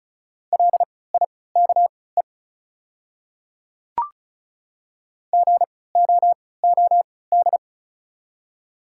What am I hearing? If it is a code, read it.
Morse code, heard as a keyed tone, about 36 wpm, spaced as for 20 wpm, 700 Hz. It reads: LIKE GOOD
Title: 3x Character Spacing - Top 500 Words - 35wpm